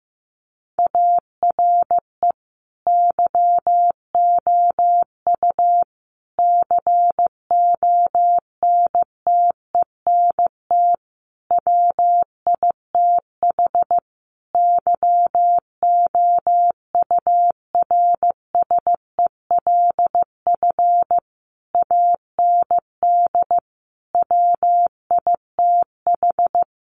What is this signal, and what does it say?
Morse code, heard as a keyed tone, about 15 wpm, 700 Hz. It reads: ARE YOU CONTENT WITH YOURSELF AND WITH
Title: Sentences from Top 900 Words 15wpm